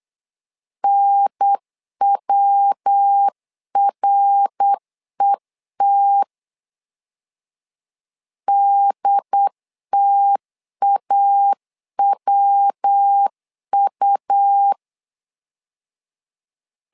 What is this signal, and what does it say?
Morse code, heard as a keyed tone, about 8 wpm, 800 Hz. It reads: NWRET DTAWU
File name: LEZIONI / 3 MWRO / 3 MWR 8wpm 2444